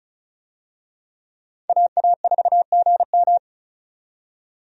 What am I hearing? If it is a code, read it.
Morse code, heard as a keyed tone, about 34 wpm, 700 Hz. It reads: AA4GM
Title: USA Call Signs - 5 Characters - 35wpm